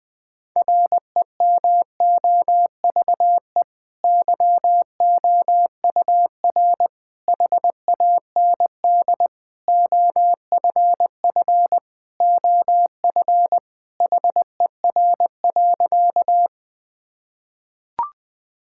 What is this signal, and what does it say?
Morse code, heard as a keyed tone, about 19 wpm, 700 Hz. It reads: REMOVE YOUR HAND OFF OF HER.
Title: Sentences from Top 1000 Words - Review Part 5 - 20wpm